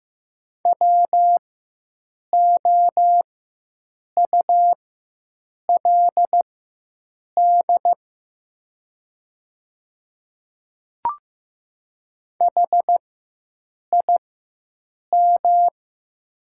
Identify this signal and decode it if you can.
Morse code, heard as a keyed tone, about 15 wpm, spaced as for 7 wpm, 700 Hz. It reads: WOULD HIM
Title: 4x Character Spacing - Top 100 Words - 15wpm